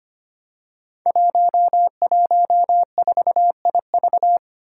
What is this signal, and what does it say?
Morse code, heard as a keyed tone, about 25 wpm, 700 Hz. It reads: 114IV
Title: Five Challenging Letter-Number Combinations 25wpm